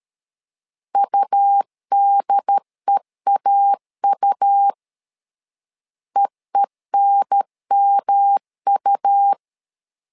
Morse code, heard as a keyed tone, about 13 wpm, 800 Hz. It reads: UDEAU EENMU
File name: LEZIONI / 3 MWRO / 3 M 12wpm 1541